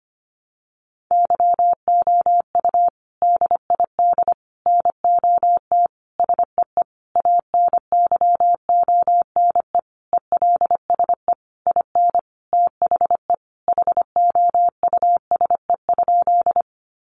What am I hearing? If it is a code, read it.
Morse code, heard as a keyed tone, about 25 wpm, 700 Hz. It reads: YOU DID NOT SEE ANYONE ELSE IN THE HOUSE?
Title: Sentences from Top 900 Words 25wpm